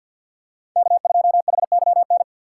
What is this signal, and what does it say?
Morse code, heard as a keyed tone, about 50 wpm, 700 Hz. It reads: K2HYN
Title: USA Call Signs - 5 Characters - 3x Character Spacing - 50wpm